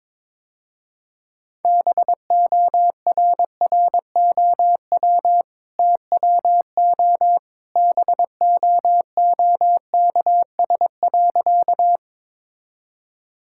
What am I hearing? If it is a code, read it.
Morse code, heard as a keyed tone, about 22 wpm, 700 Hz. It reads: BORROW TWO BOOKS.
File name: practice / 22wpm / Easy Sentences with Hints - 22wpm